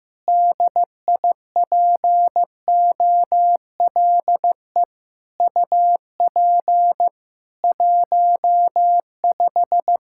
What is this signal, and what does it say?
Morse code, heard as a keyed tone, about 16 wpm, 700 Hz. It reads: DIPOLE UP 15
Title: Practice QSO - Exchange 2 - Info - 15wpm